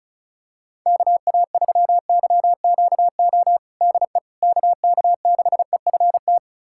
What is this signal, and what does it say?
Morse code, heard as a keyed tone, about 36 wpm, 690 Hz. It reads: KA3YQO DE KK6EFT
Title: USA Callsign DE Callsign - 6 Characters - 35wpm